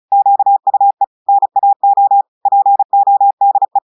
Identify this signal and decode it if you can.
Morse code, heard as a keyed tone, about 36 wpm, 800 Hz. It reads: QUE NAO PODE
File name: audios / portugais / portugais texte en clair 1 vitesse 35 mots minutes